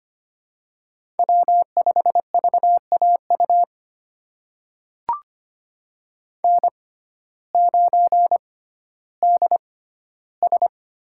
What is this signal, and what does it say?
Morse code, heard as a keyed tone, about 25 wpm, spaced as for 14 wpm, 700 Hz. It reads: W5VAU N9DS
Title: USA Call Signs - 5 Characters - 6x Character Spacing - 25wpm